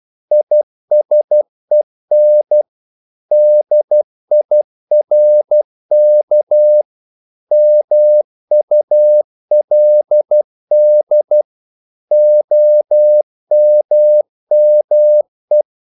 Morse code, heard as a keyed tone, about 12 wpm, 600 Hz. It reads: ISEN DIRK MULD OMME
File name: Korte ord DK 12wpm